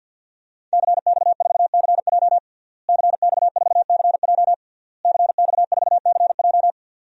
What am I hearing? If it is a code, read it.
Morse code, heard as a keyed tone, about 49 wpm, 700 Hz. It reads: CX4CJ CX4CJ CX4CJ
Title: DXCC-100 Call Signs - 5 Characters - Repeated 3x - 50wpm